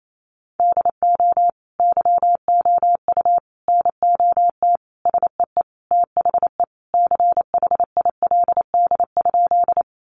Morse code, heard as a keyed tone, about 28 wpm, 700 Hz. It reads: DO YOU NOT SEE THE CHILD?
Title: Sentences from Top 800 Words 28wpm